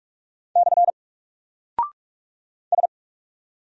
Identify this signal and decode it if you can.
Morse code, heard as a keyed tone, about 43 wpm, 700 Hz. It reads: / S
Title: Single Character (Letter:Number:Punctuation:Prosign) Alt 45wpm